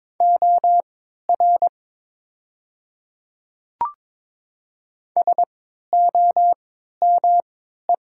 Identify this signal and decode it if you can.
Morse code, heard as a keyed tone, about 22 wpm, spaced as for 12 wpm, 700 Hz. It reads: OR SOME
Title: 3x Character Spacing - Top 100 Words - 22wpm